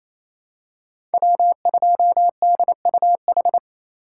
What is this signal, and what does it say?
Morse code, heard as a keyed tone, about 28 wpm, 700 Hz. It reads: W2DUH
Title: USA Call Signs - 5 Characters - 6x Character Spacing - 28wpm